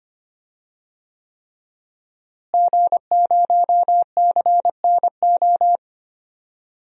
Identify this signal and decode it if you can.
Morse code, heard as a keyed tone, about 25 wpm, 700 Hz. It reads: G0CNO
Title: DXCC-100 Call Signs - 5 Characters - Repeated 3x - 25wpm